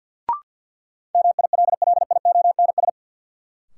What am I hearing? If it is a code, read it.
Morse code, heard as a keyed tone, about 50 wpm, 700 Hz. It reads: MILLIONS
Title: Sets of 5 Words - New Word Drill - 50wpm